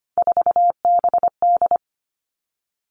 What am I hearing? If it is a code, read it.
Morse code, heard as a keyed tone, about 25 wpm, 700 Hz. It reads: 4BD
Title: Three Challenging Letter-Number Combinations 25wpm